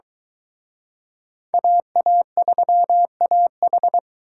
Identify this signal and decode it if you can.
Morse code, heard as a keyed tone, about 23 wpm, 700 Hz. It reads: AA3AH